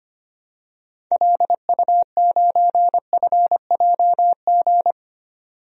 Morse code, heard as a keyed tone, about 25 wpm, 700 Hz. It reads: LU9FJG
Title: DXCC-100 Call Signs - 6 Characters - 25wpm